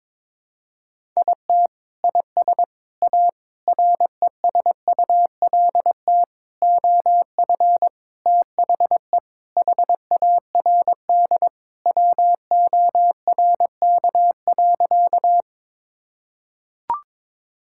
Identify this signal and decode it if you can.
Morse code, heard as a keyed tone, about 22 wpm, 700 Hz. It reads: IT IS A RESULT OF THE HARD WORK.
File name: practice / 22wpm / Sentences with 4-Word Phrases - 22wpm